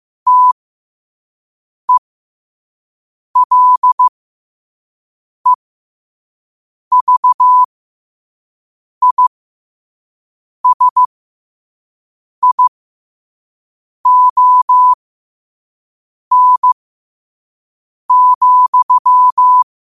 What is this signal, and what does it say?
Morse code, heard as a keyed tone, about 15 wpm, spaced as for 5 wpm, 1000 Hz. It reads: TELEVISION,
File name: qotdfiles / 5wpm